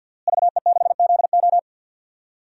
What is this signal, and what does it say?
Morse code, heard as a keyed tone, about 52 wpm, 700 Hz. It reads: VE6ZO
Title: DXCC-100 Call Signs - 5 Characters - Repeated 2x - 50wpm